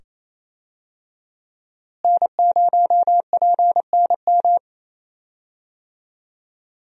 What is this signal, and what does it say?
Morse code, heard as a keyed tone, about 28 wpm, 700 Hz. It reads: N0PNM
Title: USA Call Signs - 5 Characters - Repeated 2x - 28wpm